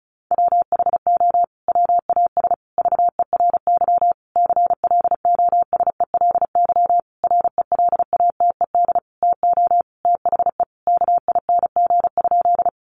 Morse code, heard as a keyed tone, about 34 wpm, 700 Hz. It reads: WHO WAS VERY CLOSELY RELATED TO THE KING?
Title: Sentences from Top 1400 Words 35wpm